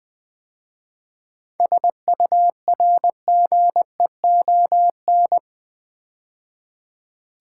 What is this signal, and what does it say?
Morse code, heard as a keyed tone, about 20 wpm, 700 Hz. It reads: SURGEON
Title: 7 Letter Words - First 3 Letters Given - 20wpm